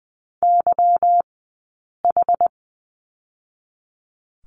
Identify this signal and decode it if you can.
Morse code, heard as a keyed tone, about 19 wpm, spaced as for 8 wpm, 700 Hz. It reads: YH
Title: 144 Lesson 24d - Letter y - Two Characters 20wpm